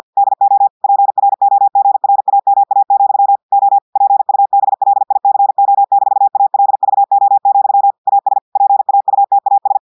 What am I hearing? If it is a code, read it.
Morse code, heard as a keyed tone, about 50 wpm, 800 Hz. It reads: DO PROGRAMA, O PUBLICO-ALVO, AS PAUTAS
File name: portugais texte en clair 2 vitesse 50 mots minutes